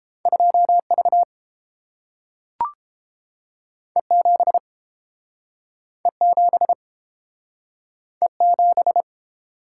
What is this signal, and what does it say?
Morse code, heard as a keyed tone, about 30 wpm, 700 Hz. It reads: 2V E7 E7 E7